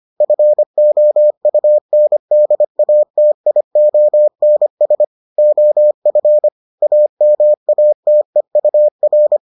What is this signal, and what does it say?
Morse code, heard as a keyed tone, about 25 wpm, 600 Hz. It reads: FOUNDATIONS OF AMATEUR